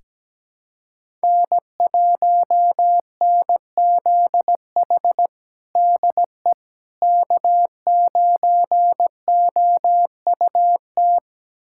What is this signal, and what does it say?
Morse code, heard as a keyed tone, about 17 wpm, 700 Hz. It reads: N1NZH DE K9OUT
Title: USA Callsign DE Callsign - 5 Characters - 17wpm